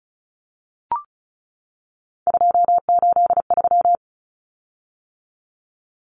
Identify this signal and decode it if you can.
Morse code, heard as a keyed tone, about 34 wpm, 700 Hz. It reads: 283
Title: Common Cut Numbers - 3-Digit 35wpm